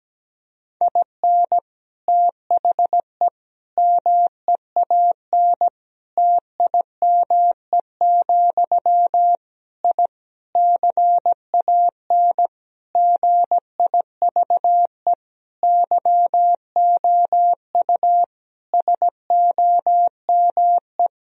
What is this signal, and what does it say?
Morse code, heard as a keyed tone, about 17 wpm, 700 Hz. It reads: IN THE MEAN TIME, I CAN GIVE YOU SOME
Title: Sentences from Top 1000 Words - Review Part 3 - 17wpm